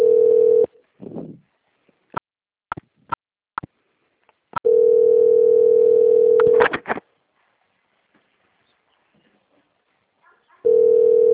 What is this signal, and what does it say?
Ringback tone (North America), 440 + 480 Hz.
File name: out- - - - - .32110